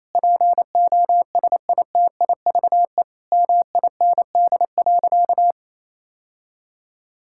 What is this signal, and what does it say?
Morse code, heard as a keyed tone, about 27 wpm, 700 Hz. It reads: POSITIVE MIND.
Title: Sentences from Top 1000 Words - Review Part 5 - 28wpm